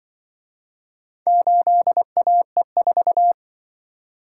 Morse code, heard as a keyed tone, about 24 wpm, 700 Hz. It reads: 8AE4
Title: Speed-Racing - Four Letter-Number Combinations - 20wpm